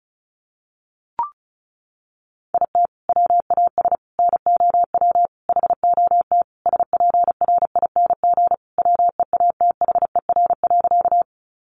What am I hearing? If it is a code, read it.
Morse code, heard as a keyed tone, about 34 wpm, 700 Hz. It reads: IT WAS NOW HOT SPRING WEATHER.
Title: Sentences from Top 1200 Words 35wpm